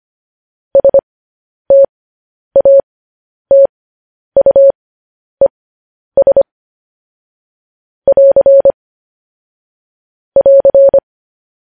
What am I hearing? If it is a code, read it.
Morse code, heard as a keyed tone, about 25 wpm, spaced as for 10 wpm, 550 Hz. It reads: STATUES <AR> <AR>
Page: news-headlines-in-morse-code-at-10-wpm